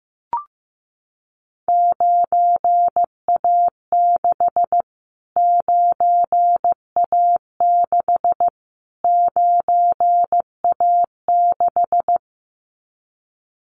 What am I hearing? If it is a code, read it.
Morse code, heard as a keyed tone, about 15 wpm, 700 Hz. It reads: 9A6 9A6 9A6
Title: Callsign Prefixes for Top 25 DXCC - Repeated 3x - 15wpm